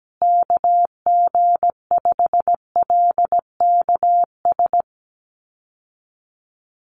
Morse code, heard as a keyed tone, about 17 wpm, 700 Hz. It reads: KG5LKS
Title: USA Call Signs Repeated 3x 17wpm